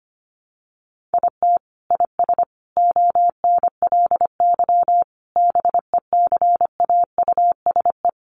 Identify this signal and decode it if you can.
Morse code, heard as a keyed tone, about 25 wpm, 700 Hz. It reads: IT IS ONLY BECAUSE
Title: Sentences from Top 1400 Words 25wpm